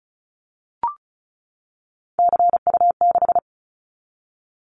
Morse code, heard as a keyed tone, about 36 wpm, 700 Hz. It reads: CU6
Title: Callsign Prefixes for Top 50 DXCC - 35wpm